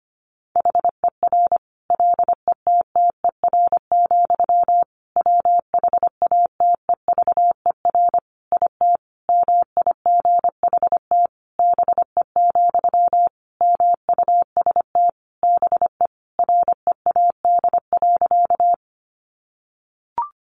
Morse code, heard as a keyed tone, about 25 wpm, 700 Hz. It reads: HER LETTER, WHATEVER IT MIGHT BE, MUST BE READ.
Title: Sentences from Top 1100 Words 25wpm